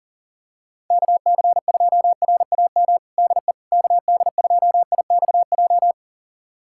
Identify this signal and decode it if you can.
Morse code, heard as a keyed tone, about 40 wpm, 690 Hz. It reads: KC2RAM DE KD2IXJ
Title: USA Callsign DE Callsign - 6 Characters - 40wpm